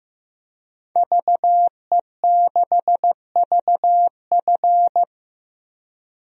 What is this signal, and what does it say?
Morse code, heard as a keyed tone, about 14 wpm, 700 Hz. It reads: VE6VF
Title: DXCC-100 Call Signs - 5 Characters - Repeated 2x - 15wpm